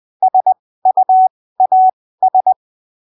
Morse code, heard as a keyed tone, about 20 wpm, spaced as for 15 wpm, 750 Hz. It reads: SUAS
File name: arquivos / texto / texto 23 20ppm